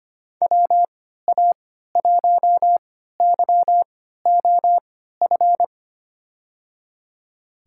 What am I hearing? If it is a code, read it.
Morse code, heard as a keyed tone, about 25 wpm, spaced as for 14 wpm, 700 Hz. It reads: WA1YOF
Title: USA Call Signs - 6 Characters - 3x Character Spacing - 25wpm